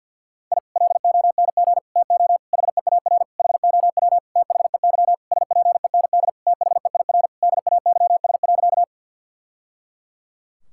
Morse code, heard as a keyed tone, about 46 wpm, 700 Hz. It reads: I LONG TO HEAR HOW THEY SPEND THEIR DAYS.
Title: Sentences from Top 1000 Words - Review Part 5 - 50wpm